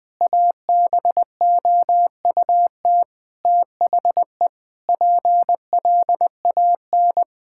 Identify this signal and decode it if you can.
Morse code, heard as a keyed tone, about 20 wpm, 700 Hz. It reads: ABOUT THE PLAN
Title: Sentences with 8-Word Phrases - 20wpm